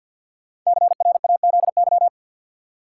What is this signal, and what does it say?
Morse code, heard as a keyed tone, about 52 wpm, 700 Hz. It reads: CRAZY
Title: Five Letter Words - 50wpm